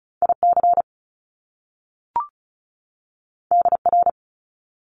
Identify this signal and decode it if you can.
Morse code, heard as a keyed tone, about 36 wpm, 700 Hz. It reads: IC DR
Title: Top Two Letter Combinations in Words 35wpm